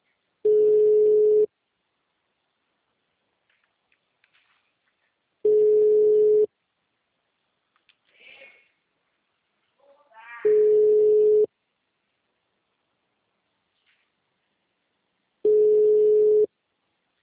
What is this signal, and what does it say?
Ringback tone (Europe, 425 Hz).